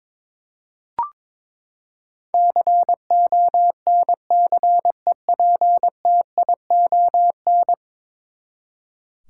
Morse code, heard as a keyed tone, about 22 wpm, 700 Hz. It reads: CONCEPTION
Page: morse-code-ninja-practice-22wpm